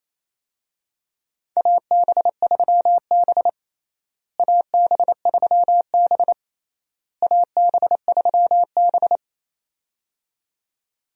Morse code, heard as a keyed tone, about 28 wpm, 700 Hz. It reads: AB3B AB3B AB3B
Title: USA Call Signs - 4 Characters - Repeated 3x - 28wpm